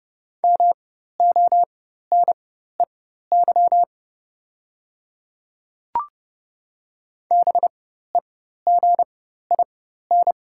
Morse code, heard as a keyed tone, about 30 wpm, spaced as for 14 wpm, 700 Hz. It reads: MONEY BEGIN
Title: 4x Character Spacing - Top 500 Words - 30wpm